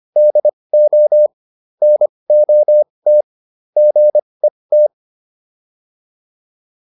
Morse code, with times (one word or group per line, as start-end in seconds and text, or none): 0.2-1.3 DO
1.8-3.2 NOT
3.8-4.9 GET